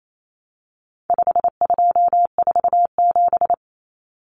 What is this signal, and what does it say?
Morse code, heard as a keyed tone, about 28 wpm, 700 Hz. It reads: 5247
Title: Common Cut Numbers - 4-Digit 28wpm